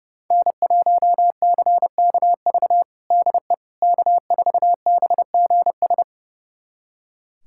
Morse code, heard as a keyed tone, about 32 wpm, 700 Hz. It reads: N1CKV DE K4BGS
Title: USA Callsign DE Callsign - 5 Characters - 30wpm